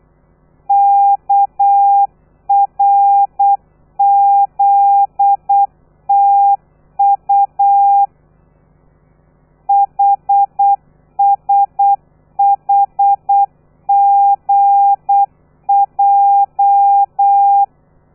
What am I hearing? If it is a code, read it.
Morse code, heard as a keyed tone, about 8 wpm, 800 Hz. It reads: KRZTU HSHGJ
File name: telegrafia / i2tfj / Lez13 35